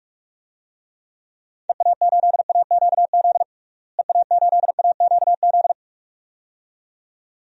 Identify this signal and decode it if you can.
Morse code, heard as a keyed tone, about 46 wpm, 700 Hz. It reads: EA8AQZ EA8AQZ
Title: DXCC-100 Call Signs - 6 Characters - Repeated 2x - 45wpm